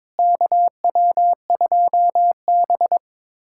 Morse code, heard as a keyed tone, about 22 wpm, 700 Hz. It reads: KW2B
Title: USA Callsign DE Callsign - 4 Characters - 22wpm